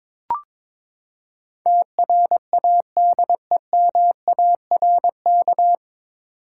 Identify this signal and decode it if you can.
Morse code, heard as a keyed tone, about 22 wpm, 700 Hz. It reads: TRADEMARK
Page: Morse Code Ninja practice 22wpm